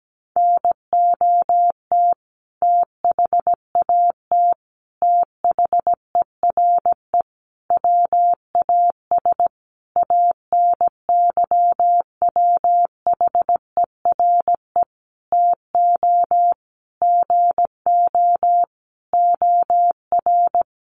Morse code, heard as a keyed tone, about 17 wpm, 700 Hz. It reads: NOT THAT THERE WAS ANYWHERE TO GO OR
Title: Sentences from Top 1400 Words 17wpm